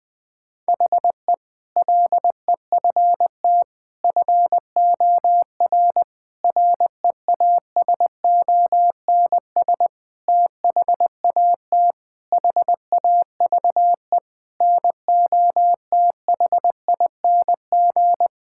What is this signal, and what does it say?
Morse code, heard as a keyed tone, about 20 wpm, 700 Hz. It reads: HE LEFT FOR REASONS THAT HAVE NOTHING
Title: Sentences with 8-Word Phrases - 20wpm